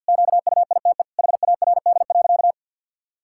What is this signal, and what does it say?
Morse code, heard as a keyed tone, about 50 wpm, 700 Hz. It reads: QUITE HARD.
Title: Pangram Challenge v2 - 50wpm